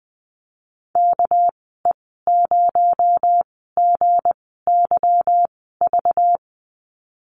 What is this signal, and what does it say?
Morse code, heard as a keyed tone, about 20 wpm, spaced as for 15 wpm, 700 Hz. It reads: KE0GYV
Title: 245 Lesson 39g - Letter z - Call Signs 20wpm